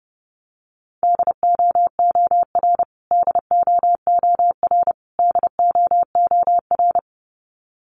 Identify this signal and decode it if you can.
Morse code, heard as a keyed tone, about 30 wpm, 700 Hz. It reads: DOOR DOOR DOOR
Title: Top 300 Words Repeated 30wpm